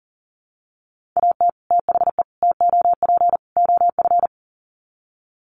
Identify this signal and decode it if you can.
Morse code, heard as a keyed tone, about 40 wpm, 700 Hz. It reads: AT THE TOP OF
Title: Sets of 4 Words 40wpm